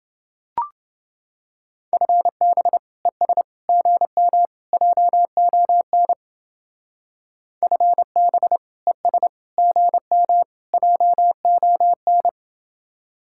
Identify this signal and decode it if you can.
Morse code, heard as a keyed tone, about 29 wpm, 700 Hz. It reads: FB ES GM JON FB ES GM JON